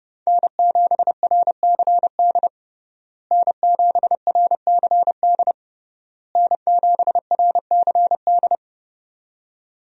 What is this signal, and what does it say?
Morse code, heard as a keyed tone, about 30 wpm, 700 Hz. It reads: N7RCD N7RCD N7RCD
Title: USA Call Signs - 5 Characters - Repeated 3x - 30wpm